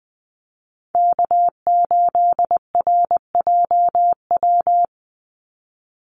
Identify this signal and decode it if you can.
Morse code, heard as a keyed tone, about 20 wpm, 700 Hz. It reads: K8RJW
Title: Call Signs - 20wpm